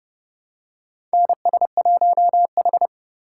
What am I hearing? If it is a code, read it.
Morse code, heard as a keyed tone, about 30 wpm, 700 Hz. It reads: NS1H
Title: USA Call Signs - 4 Characters - 3x Character Spacing - 30wpm